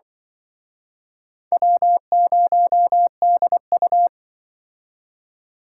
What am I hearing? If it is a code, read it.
Morse code, heard as a keyed tone, about 25 wpm, 700 Hz. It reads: W0DU
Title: Speed Racing - USA 1x2 Call Signs 20wpm